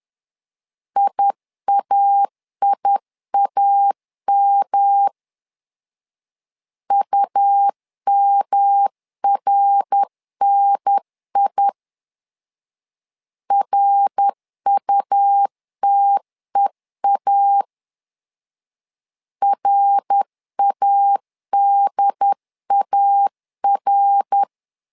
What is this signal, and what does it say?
Morse code, heard as a keyed tone, about 10 wpm, 800 Hz. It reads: IAIAM UMRNI RUTEA RADAR
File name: LEZIONI / 3 MWRO / 3 MWR 10wpm 1952